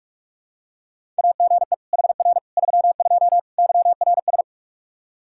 Morse code, heard as a keyed tone, about 41 wpm, 700 Hz. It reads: AGE HR 32 YRS
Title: Practice QSO - Exchange 3 - More Info - 45wpm